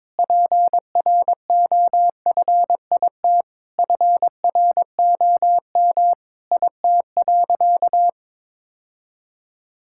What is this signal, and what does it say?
Morse code, heard as a keyed tone, about 22 wpm, 700 Hz. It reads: PROFIT FROM IT.
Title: Sentences from Top 1000 Words - Review Part 5 - 22wpm